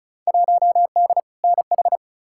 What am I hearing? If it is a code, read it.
Morse code, heard as a keyed tone, about 34 wpm, 700 Hz. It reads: 1D NH
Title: Field Day Exchanges - 35wpm